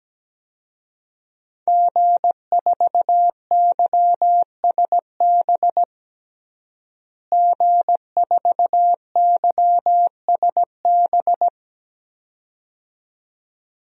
Morse code, heard as a keyed tone, about 17 wpm, 700 Hz. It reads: G4YSB G4YSB
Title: DXCC-100 Call Signs - 5 Characters - Repeated 2x - 17wpm